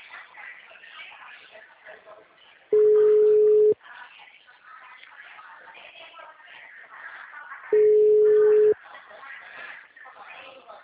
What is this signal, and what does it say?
Ringback tone (Europe, 425 Hz), with speech.